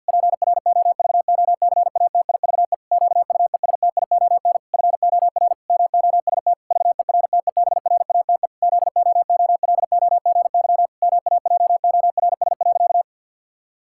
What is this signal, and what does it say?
Morse code, heard as a keyed tone, about 50 wpm, 700 Hz. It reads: PROVOCATIVE QUESTION FOR MOST VERTEBRATE ZOOLOGY MAJORS.
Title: Pangram Challenge v2 - 50wpm